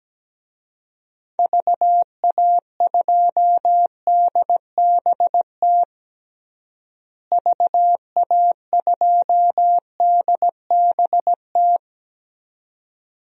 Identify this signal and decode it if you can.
Morse code, heard as a keyed tone, about 17 wpm, 700 Hz. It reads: VA2DBT VA2DBT
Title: DXCC-100 Call Signs - 6 Characters - Repeated 2x - 17wpm